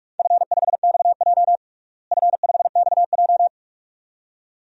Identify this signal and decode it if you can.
Morse code, heard as a keyed tone, about 46 wpm, 700 Hz. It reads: F5XJ F5XJ
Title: DXCC-100 Call Signs - 4 Characters - Repeated 2x - 45wpm